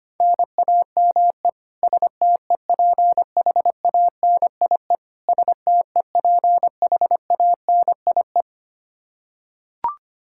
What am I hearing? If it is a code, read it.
Morse code, heard as a keyed tone, about 25 wpm, 700 Hz. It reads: NAME STEPHANIE STEPHANIE
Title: Practice QSO - Exchange 1 - Beginning - 25wpm